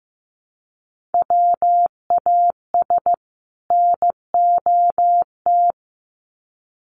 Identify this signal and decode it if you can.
Morse code, heard as a keyed tone, about 15 wpm, 700 Hz. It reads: WAS NOT
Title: Sets of 2 Words 15wpm